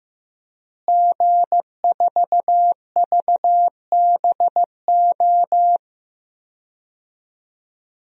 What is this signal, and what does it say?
Morse code, heard as a keyed tone, about 15 wpm, 700 Hz. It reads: G4VBO
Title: DXCC-100 Call Signs - 5 Characters - Repeated 3x - 15wpm